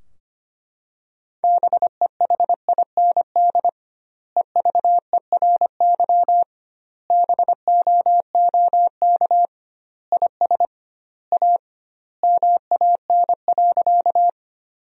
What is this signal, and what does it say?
Morse code, heard as a keyed tone, about 25 wpm, 700 Hz. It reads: BEHIND EVERY BOOK IS A MAN.
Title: 2x Word-Spacing - Sentences from Top 500 Words - 25wpm